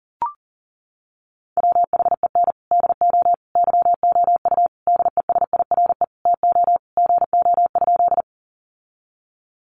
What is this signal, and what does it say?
Morse code, heard as a keyed tone, about 37 wpm, 700 Hz. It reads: WHEN DO YOU DESIRE TO GO?
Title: Sentences from Top 1100 Words 40wpm